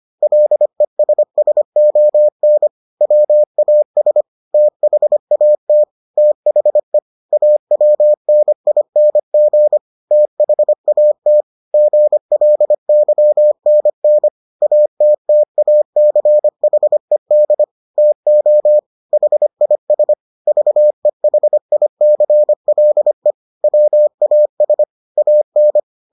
Morse code, heard as a keyed tone, about 25 wpm, 600 Hz. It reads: LESSON WAS THAT THE AWNING THAT GLYNN ATTACHED TO HIS VEHICLE WAS AN